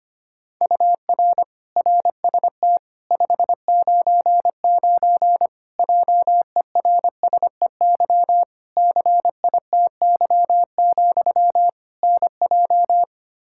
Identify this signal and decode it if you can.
Morse code, heard as a keyed tone, about 25 wpm, 700 Hz. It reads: UR RST 599 JERSEY CITY, NJ
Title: Practice QSO - Exchange 1 - Beginning - 25wpm